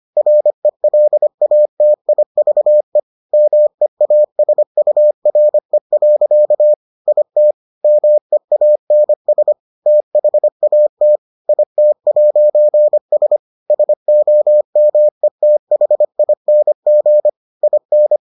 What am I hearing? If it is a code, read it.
Morse code, heard as a keyed tone, about 25 wpm, 600 Hz. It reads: RELATIVE MEASURE. IT MEANS THAT IT'S SOMETHING IN